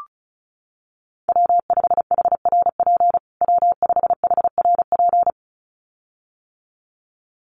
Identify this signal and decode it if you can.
Morse code, heard as a keyed tone, about 34 wpm, 700 Hz. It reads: W5HRP W5HRP
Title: USA Call Signs Repeated 2x 35wpm